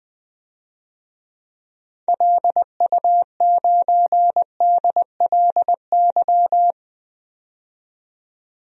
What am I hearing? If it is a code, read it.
Morse code, heard as a keyed tone, about 20 wpm, 700 Hz. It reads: LU9DLY
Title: DXCC-100 Call Signs - 6 Characters - 20wpm